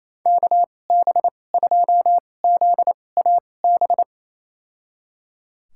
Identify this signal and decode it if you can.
Morse code, heard as a keyed tone, about 28 wpm, spaced as for 20 wpm, 700 Hz. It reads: KB2ZAB
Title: USA Call Signs - 6 Characters - 2x Character Spacing - 28wpm